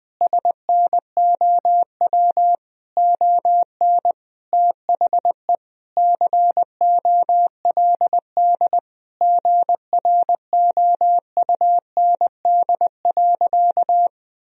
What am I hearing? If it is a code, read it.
Morse code, heard as a keyed tone, about 21 wpm, 700 Hz. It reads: SNOW ON THE COLD GROUND.
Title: Sentences from Top 1000 Words - Review Part 3 - 20wpm